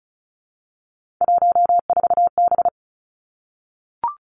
Morse code, heard as a keyed tone, about 36 wpm, 700 Hz. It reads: 14B
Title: Three Challenging Letter-Number Combinations 35wpm